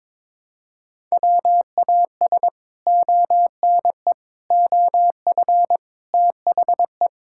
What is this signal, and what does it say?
Morse code, heard as a keyed tone, about 21 wpm, 700 Hz. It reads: WAS ONE OF THE
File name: practice / 22wpm / Build A Phrase - 5-Word Phrases - 22wpm